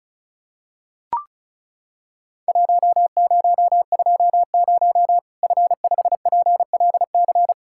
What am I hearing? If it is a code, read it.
Morse code, heard as a keyed tone, about 35 wpm, 700 Hz. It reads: 1020 F5PLC
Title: Worked All Europe DX Contest - QTC Exchange - 35wpm